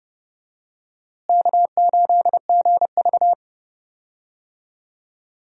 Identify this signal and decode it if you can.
Morse code, heard as a keyed tone, about 30 wpm, 700 Hz. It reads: K8GV
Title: USA Call Signs - 4 Characters - Repeated 2x - 30wpm